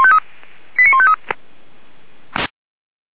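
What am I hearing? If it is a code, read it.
Five-tone selective call (ZVEI), heard as quick runs of five tones, tones of about 70 ms. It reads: 89153
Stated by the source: ZVEI five-tone selective call 89484 received by SDR.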